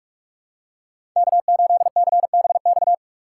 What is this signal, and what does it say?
Morse code, heard as a keyed tone, about 46 wpm, 700 Hz. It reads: K8CBX
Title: USA Call Signs - 5 Characters - 6x Character Spacing - 45wpm